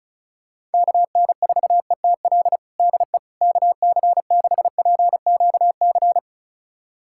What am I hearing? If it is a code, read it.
Morse code, heard as a keyed tone, about 36 wpm, 700 Hz. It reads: KN4ETL DE KC6PQC
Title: USA Callsign DE Callsign 35wpm